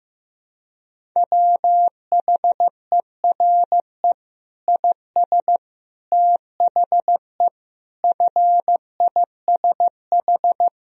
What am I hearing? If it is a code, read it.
Morse code, heard as a keyed tone, about 15 wpm, 700 Hz. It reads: WHERE IS THE FISH
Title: Sentences from Top 1000 Words - Review Part 3 - 15wpm